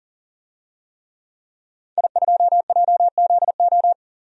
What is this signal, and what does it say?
Morse code, heard as a keyed tone, about 40 wpm, 700 Hz. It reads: I2JZO
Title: DXCC-100 Call Signs - 5 Characters - 40wpm